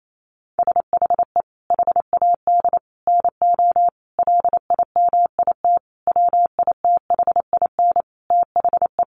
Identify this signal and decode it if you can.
Morse code, heard as a keyed tone, about 28 wpm, 700 Hz. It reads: SHE HAD NO LIMIT WITHIN THE
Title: Sentences from Top 1500 Words 28wpm